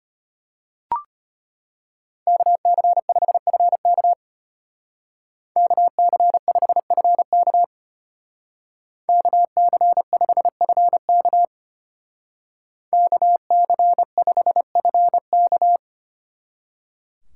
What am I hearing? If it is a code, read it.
Morse code, heard as a keyed tone, about 32 wpm, 700 Hz. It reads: KC5FK KC5FK KC5FK KC5FK